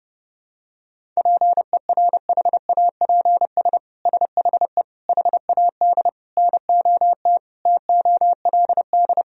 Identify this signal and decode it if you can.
Morse code, heard as a keyed tone, about 30 wpm, 700 Hz. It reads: PERHAPS SHE HAD NOT TOLD
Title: Sentences from Top 1000 Words - Review Part 1 - 30wpm